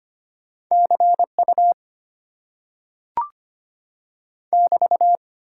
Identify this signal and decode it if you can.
Morse code, heard as a keyed tone, about 25 wpm, 700 Hz. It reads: CU <BT>
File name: Top 90 QSO Elements - 8x Character Spacing - 25wpm